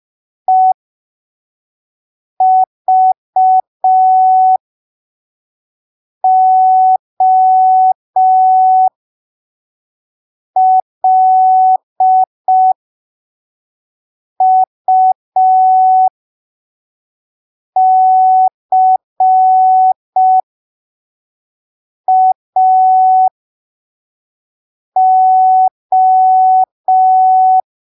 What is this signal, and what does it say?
Morse code, heard as a keyed tone, about 5 wpm, spaced as for 3 wpm, 750 Hz. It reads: EVOLUCAO
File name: arquivos / texto / texto 05 5ppm